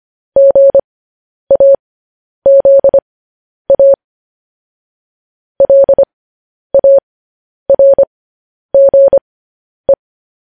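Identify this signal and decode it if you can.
Morse code, heard as a keyed tone, about 25 wpm, spaced as for 10 wpm, 550 Hz. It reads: GAZA LARGE
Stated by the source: Morse code news headlines broadcast at 10 wpm.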